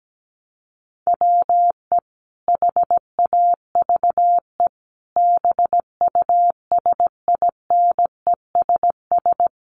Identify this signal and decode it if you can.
Morse code, heard as a keyed tone, about 17 wpm, 700 Hz. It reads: WE HAVE BUSINESS
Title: Sentences from Top 600 Words 17wpm